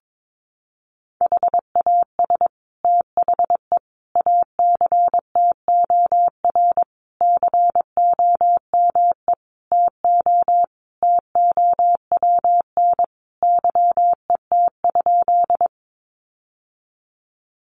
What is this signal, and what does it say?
Morse code, heard as a keyed tone, about 22 wpm, 700 Hz. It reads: HAS THE ACTOR COME TO TOWN YET?
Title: Sentences from Top 1500 Words 22wpm